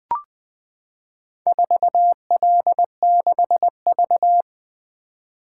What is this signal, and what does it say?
Morse code, heard as a keyed tone, about 21 wpm, 700 Hz. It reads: 4L6V
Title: Four Letter-Number Combinations - 20wpm